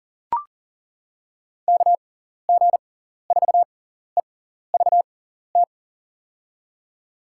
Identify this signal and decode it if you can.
Morse code, heard as a keyed tone, about 40 wpm, spaced as for 14 wpm, 700 Hz. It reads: KG4EVT
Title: USA Call Signs - 6x Character Spacing - 40wpm